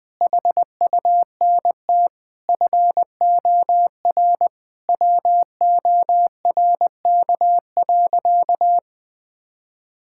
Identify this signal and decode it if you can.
Morse code, heard as a keyed tone, about 20 wpm, 700 Hz. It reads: HUNT FOR WORK.
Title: Sentences from Top 1600 Words 20wpm